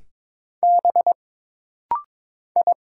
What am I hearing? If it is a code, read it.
Morse code, heard as a keyed tone, about 22 wpm, 700 Hz. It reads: B I
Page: This rapid-fire timing is geared to push your ICR (Instant Character Recognition) skills.